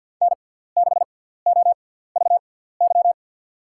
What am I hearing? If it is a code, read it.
Morse code, heard as a keyed tone, about 52 wpm, spaced as for 17 wpm, 700 Hz. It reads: N6OVY
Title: USA Call Signs - 6x Character Spacing - 50wpm